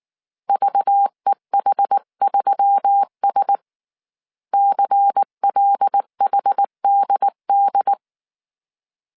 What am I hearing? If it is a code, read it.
Morse code, heard as a keyed tone, about 19 wpm, 800 Hz. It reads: VEH3S CLHDD
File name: CW / DET / 1 6 1500C S C 18wpm 2356